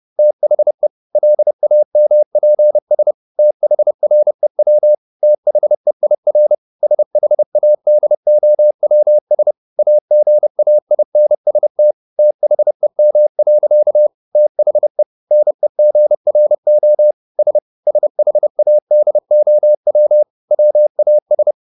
Morse code, heard as a keyed tone, about 30 wpm, 600 Hz. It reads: THE LAMPS THREW THEIR SHADOWS AGAINST THEM. THE NEGRO S SHADOW WAS